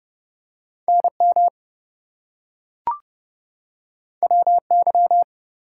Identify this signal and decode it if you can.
Morse code, heard as a keyed tone, about 30 wpm, 700 Hz. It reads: NM WY
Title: US States and Canadian Provinces 30wpm